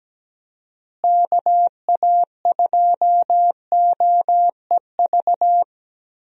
Morse code, heard as a keyed tone, about 17 wpm, 700 Hz. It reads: KA2OEV
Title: USA Call Signs - 6 Characters - Repeated 2x - 17wpm